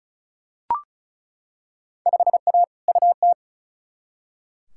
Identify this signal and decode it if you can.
Morse code, heard as a keyed tone, about 36 wpm, 700 Hz. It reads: 5A UT
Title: Field Day Exchanges - 35wpm